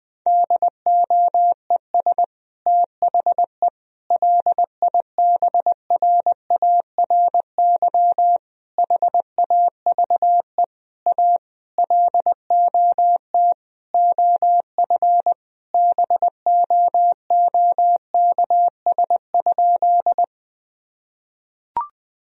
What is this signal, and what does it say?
Morse code, heard as a keyed tone, about 20 wpm, 700 Hz. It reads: DOES THE LIBRARY HAVE A LOT OF BOOKS?
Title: Sentences from Top 1000 Words - Review Part 4 - 20wpm